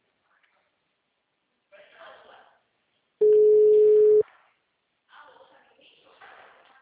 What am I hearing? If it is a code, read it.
Ringback tone (Europe, 425 Hz), with speech.